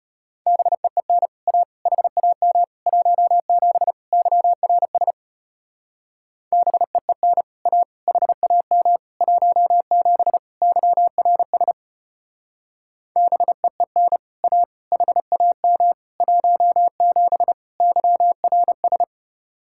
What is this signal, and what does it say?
Morse code, heard as a keyed tone, about 31 wpm, 700 Hz. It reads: BEEN A HAM 17 YRS BEEN A HAM 17 YRS BEEN A HAM 17 YRS
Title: Speed-Racing Common QSO Phrases - 25wpm